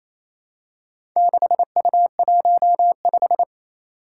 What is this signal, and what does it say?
Morse code, heard as a keyed tone, about 27 wpm, 700 Hz. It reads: 6U15
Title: Four Challenging Letter-Number Combinations 28wpm